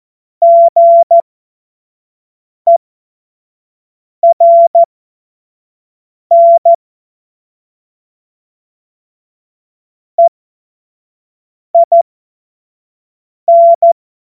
Morse code, heard as a keyed tone, about 14 wpm, spaced as for 5 wpm, 680 Hz. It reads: GERN EIN